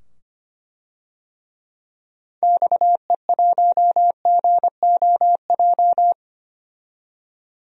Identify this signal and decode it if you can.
Morse code, heard as a keyed tone, about 25 wpm, 700 Hz. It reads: XE1GOJ
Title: DXCC-100 Call Signs - 6 Characters - 25wpm